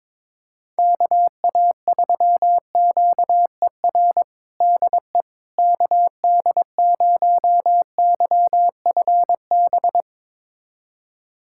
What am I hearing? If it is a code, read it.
Morse code, heard as a keyed tone, about 21 wpm, 700 Hz. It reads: KA3QER DE KD0YFB
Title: USA Callsign DE Callsign - 6 Characters - 22wpm